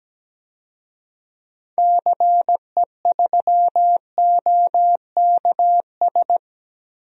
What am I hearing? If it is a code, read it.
Morse code, heard as a keyed tone, about 17 wpm, 700 Hz. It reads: CE3OKS
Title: DXCC-100 Call Signs - 6 Characters - Repeated 2x - 17wpm